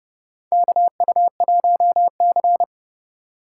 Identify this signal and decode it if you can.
Morse code, heard as a keyed tone, about 30 wpm, 700 Hz. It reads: KU1C
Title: USA Call Signs - 4 Characters - 2x Character Spacing - 30wpm